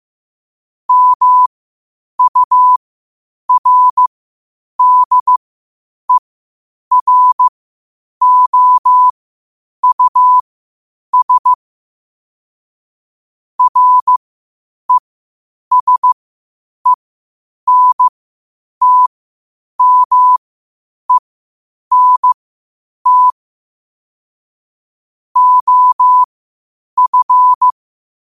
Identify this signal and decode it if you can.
Morse code, heard as a keyed tone, about 15 wpm, spaced as for 8 wpm, 1000 Hz. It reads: MURDEROUS RESENTMENT OF
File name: qotdfiles / 8wpm